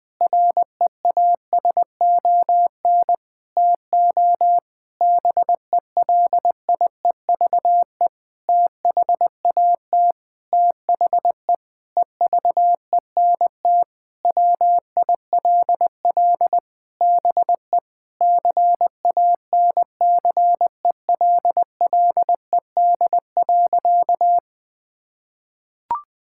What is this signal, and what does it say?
Morse code, heard as a keyed tone, about 20 wpm, 700 Hz. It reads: REASON TO BELIEVE THAT THE EVENT WILL BE CANCELLED.
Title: Sentences with 8-Word Phrases - 20wpm